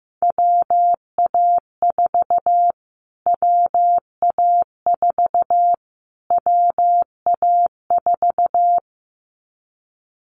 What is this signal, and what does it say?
Morse code, heard as a keyed tone, about 15 wpm, 700 Hz. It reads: WA4 WA4 WA4
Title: Callsign Prefixes for Top 25 DXCC - Repeated 3x - 15wpm